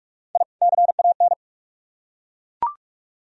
Morse code, heard as a keyed tone, about 43 wpm, 700 Hz. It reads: I CAN